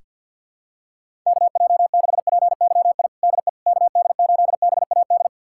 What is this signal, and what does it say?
Morse code, heard as a keyed tone, about 51 wpm, 700 Hz. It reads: KJ6PYI DE KD8BAD
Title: USA Callsign DE Callsign - 6 Characters - 50wpm